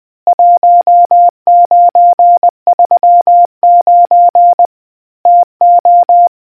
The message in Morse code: 1939 TO